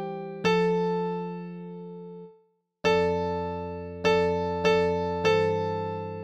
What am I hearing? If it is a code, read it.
There is no code.